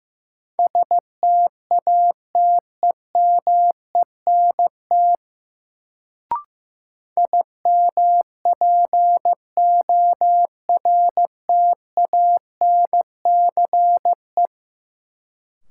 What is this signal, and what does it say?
Morse code, heard as a keyed tone, about 15 wpm, 700 Hz. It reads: STATEMENT IMPORTANCE
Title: Sentences from Top 1200 Words - New Word Drill Alt - 15wpm